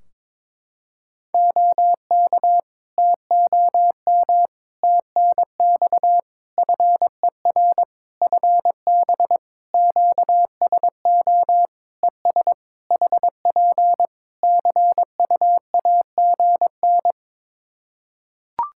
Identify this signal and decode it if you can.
Morse code, heard as a keyed tone, about 22 wpm, 700 Hz. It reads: OK TOM TNX FER FB QSO ES HP CUAGN
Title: Practice QSO - Exchange 4 - Ending - 22wpm